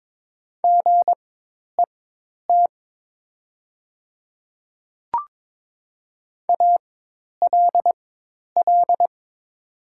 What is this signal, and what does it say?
Morse code, heard as a keyed tone, about 22 wpm, spaced as for 10 wpm, 700 Hz. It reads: GET ALL
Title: 4x Character Spacing - Top 200 Words - 22wpm